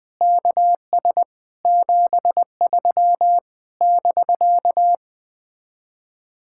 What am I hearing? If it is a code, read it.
Morse code, heard as a keyed tone, about 20 wpm, 700 Hz. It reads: KS 73 <BK>
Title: Speed-Racing Common QSO Phrases Encore - 17wpm